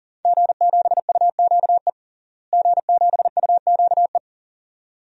Morse code, heard as a keyed tone, about 40 wpm, 700 Hz. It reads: G7UQE G7UQE
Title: DXCC-100 Call Signs - 5 Characters - Repeated 2x - 40wpm